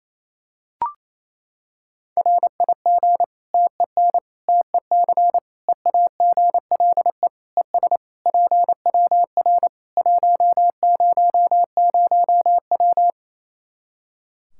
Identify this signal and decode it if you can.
Morse code, heard as a keyed tone, about 28 wpm, 700 Hz. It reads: RIG TEN TEC EAGLE ES PWR 100W
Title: Practice QSO - Exchange 2 - Info - 28wpm